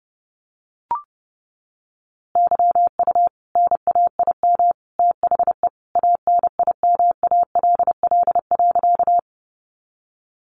Morse code, heard as a keyed tone, about 30 wpm, 700 Hz. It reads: YU NAIM THE ANIMALL.
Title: A Sound Of Thunder 30wpm